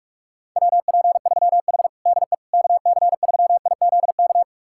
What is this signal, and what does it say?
Morse code, heard as a keyed tone, about 46 wpm, 700 Hz. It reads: WP3H DE KC3IZK
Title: USA Callsign DE Callsign 45wpm